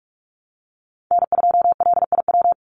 Morse code, heard as a keyed tone, about 44 wpm, 700 Hz. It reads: N2LIW
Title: USA Call Signs 45wpm